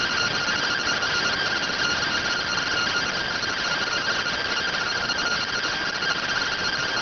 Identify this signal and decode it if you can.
Two-tone teleprinter signal, about 170 Hz shift.